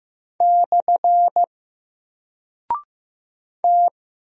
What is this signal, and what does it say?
Morse code, heard as a keyed tone, about 15 wpm, 700 Hz. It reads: / T
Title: Single Character (Letter:Number:Punctuation:Prosign) Alt 15wpm